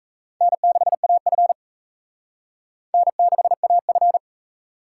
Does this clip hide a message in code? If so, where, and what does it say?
Yes: Morse code from 0.4 to 4.2 s, heard as a keyed tone, about 37 wpm, 700 Hz. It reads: N6AF N6AF